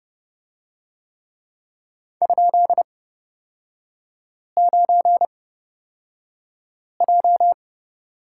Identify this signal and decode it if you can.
Morse code, heard as a keyed tone, about 30 wpm, spaced as for 5 wpm, 700 Hz. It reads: ?9J
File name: Lesson 36e - BT - Three Characters 30wpm